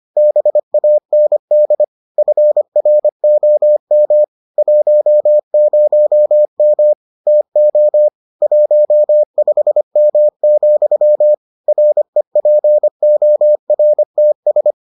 Morse code, heard as a keyed tone, about 25 wpm, 600 Hz. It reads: BAND FROM 10M TO 15M, REPORTS